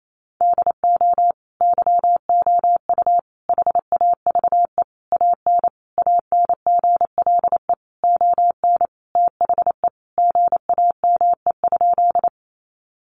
Morse code, heard as a keyed tone, about 28 wpm, 700 Hz. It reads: DO YOU HAVE AN ANGLE ON THE GAME?
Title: Sentences from Top 1500 Words 28wpm